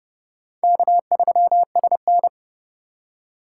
Morse code, heard as a keyed tone, about 36 wpm, 700 Hz. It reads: K3SN
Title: Speed Racing - USA Call Signs 30wpm